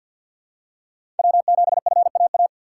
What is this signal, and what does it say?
Morse code, heard as a keyed tone, about 51 wpm, 700 Hz. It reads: W7FAA
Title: USA Call Signs - 5 Characters - 4x Character Spacing - 50wpm